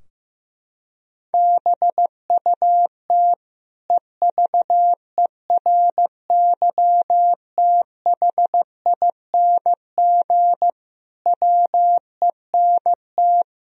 Morse code, heard as a keyed tone, about 15 wpm, 700 Hz. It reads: BUT EVERYTHING WENT
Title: Sentences from Top 1000 Words - Review Part 2 - 15wpm